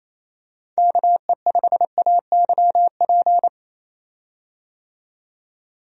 Morse code, heard as a keyed tone, about 28 wpm, 700 Hz. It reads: KE5AYP
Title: USA Call Signs - 6 Characters - 28wpm